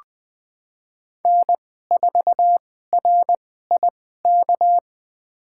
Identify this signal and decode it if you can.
Morse code, heard as a keyed tone, about 20 wpm, spaced as for 14 wpm, 700 Hz. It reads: N4RIK
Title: USA Call Signs - 5 Characters - 2x Character Spacing - 20wpm